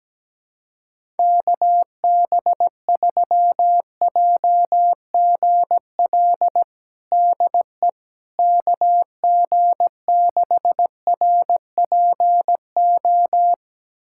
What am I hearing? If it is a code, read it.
Morse code, heard as a keyed tone, about 17 wpm, 700 Hz. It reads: KB3JGL DE KG6RPO
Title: USA Callsign DE Callsign 17wpm